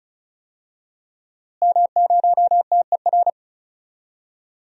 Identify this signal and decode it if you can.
Morse code, heard as a keyed tone, about 36 wpm, 700 Hz. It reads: M0TER
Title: DXCC-100 Call Signs - 5 Characters - Repeated 2x - 35wpm